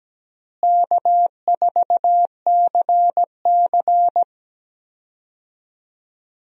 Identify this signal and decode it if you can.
Morse code, heard as a keyed tone, about 17 wpm, 700 Hz. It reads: K4CC
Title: USA Call Signs - 4 Characters - Repeated 3x - 17wpm